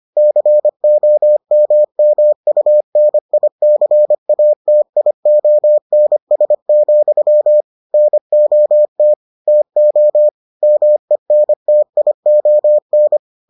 Morse code, heard as a keyed tone, about 25 wpm, 600 Hz. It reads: COMMUNICATIONS, NOT TO MENTION